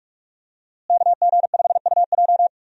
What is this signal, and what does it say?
Morse code, heard as a keyed tone, about 46 wpm, 700 Hz. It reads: KG5UJ
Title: USA Call Signs - 5 Characters - Repeated 3x - 45wpm